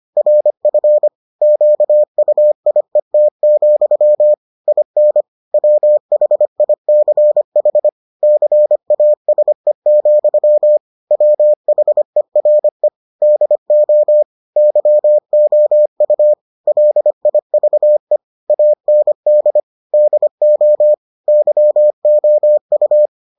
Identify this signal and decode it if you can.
Morse code, heard as a keyed tone, about 25 wpm, 600 Hz. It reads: RF QUIET, IN WHICH CASE, WHERE DO YOU LIVE AND DO YOU